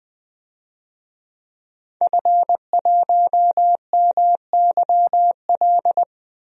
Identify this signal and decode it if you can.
Morse code, heard as a keyed tone, about 21 wpm, 700 Hz. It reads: F1MYL
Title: DXCC-100 Call Signs - 5 Characters - Repeated 3x - 20wpm